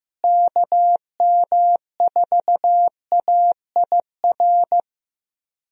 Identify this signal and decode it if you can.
Morse code, heard as a keyed tone, about 15 wpm, 700 Hz. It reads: KM4AIR
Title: USA Call Signs - 6 Characters - Repeated 2x - 15wpm